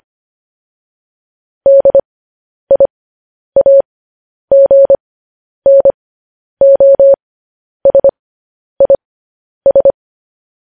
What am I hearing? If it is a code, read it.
Morse code, heard as a keyed tone, about 25 wpm, spaced as for 10 wpm, 550 Hz. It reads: DIAGNOSIS
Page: news headlines in morse code at 10 wpm